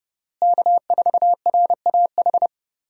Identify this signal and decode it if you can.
Morse code, heard as a keyed tone, about 30 wpm, 700 Hz. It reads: K4RAH
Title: USA Call Signs - 5 Characters - Repeated 3x - 30wpm